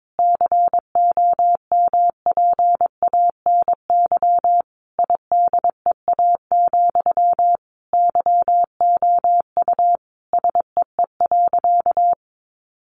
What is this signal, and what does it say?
Morse code, heard as a keyed tone, about 22 wpm, 700 Hz. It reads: COMPANY IDEA, YOU SEE.
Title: Sentences from Top 600 Words 22wpm